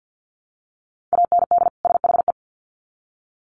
Morse code, heard as a keyed tone, about 50 wpm, 700 Hz. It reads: AND SHE
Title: Sets of 2 Words 50wpm